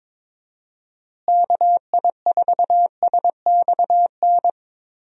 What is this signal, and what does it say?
Morse code, heard as a keyed tone, about 21 wpm, 700 Hz. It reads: KI4SXN
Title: USA Call Signs - 6 Characters - Repeated 3x - 22wpm